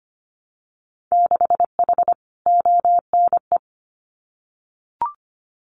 Morse code, heard as a keyed tone, about 25 wpm, 690 Hz. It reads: 6H ONE
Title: Winter Field Day Exchanges Encore 25wpm